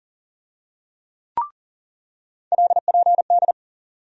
Morse code, heard as a keyed tone, about 40 wpm, 700 Hz. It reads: LPD
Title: USA Call Sign Postfixes - 3 Letters 40wpm